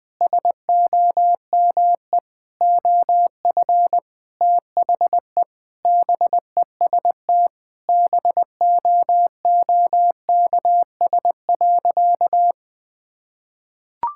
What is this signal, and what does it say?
Morse code, heard as a keyed tone, about 19 wpm, 700 Hz. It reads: SOME OF THE BEST BOOKS.
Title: Sentences with 8-Word Phrases - 20wpm